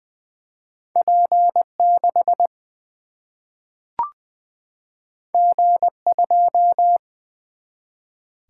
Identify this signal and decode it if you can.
Morse code, heard as a keyed tone, about 20 wpm, 700 Hz. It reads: P 6 G 2
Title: Single-Letter Single-Number 20wpm